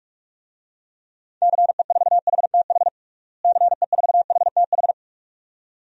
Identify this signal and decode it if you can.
Morse code, heard as a keyed tone, about 41 wpm, 700 Hz. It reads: CE4HTH CE4HTH
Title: DXCC-100 Call Signs - 6 Characters - Repeated 2x - 45wpm